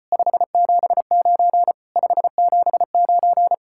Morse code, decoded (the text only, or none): 579 579